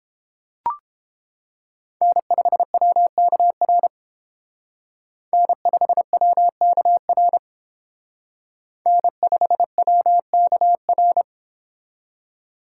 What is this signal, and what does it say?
Morse code, heard as a keyed tone, about 28 wpm, 700 Hz. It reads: N5WKR N5WKR N5WKR